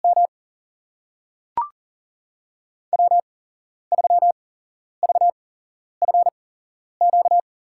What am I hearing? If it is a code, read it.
Morse code, heard as a keyed tone, about 40 wpm, spaced as for 11 wpm, 700 Hz. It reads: M W3VFQ
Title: USA Call Signs - 5 Characters - 8x Character Spacing - 40wpm